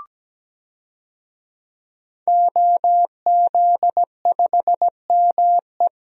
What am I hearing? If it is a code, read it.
Morse code, heard as a keyed tone, about 17 wpm, 700 Hz. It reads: OZ5ME
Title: DXCC-100 Call Signs - 5 Characters - 17wpm